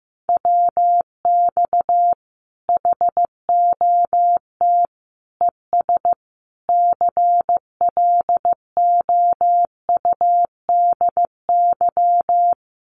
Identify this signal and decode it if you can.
Morse code, heard as a keyed tone, about 15 wpm, 700 Hz. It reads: WX HOT ES CLOUDY
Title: Common QSO Phrases 15wpm